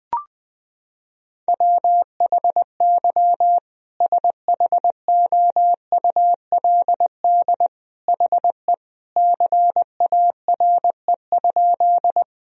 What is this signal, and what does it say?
Morse code, heard as a keyed tone, about 20 wpm, 700 Hz. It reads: WHY SHOULD HE CARE?
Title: Sentences from Top 1000 Words - Review Part 1 - 20wpm